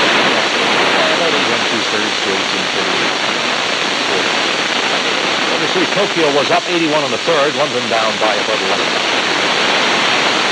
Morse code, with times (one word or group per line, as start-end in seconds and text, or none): none